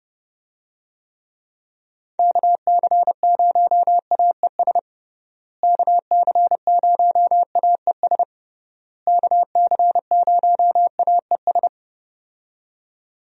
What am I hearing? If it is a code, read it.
Morse code, heard as a keyed tone, about 30 wpm, 700 Hz. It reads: KC0AES KC0AES KC0AES
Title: USA Call Signs - 6 Characters - Repeated 3x - 30wpm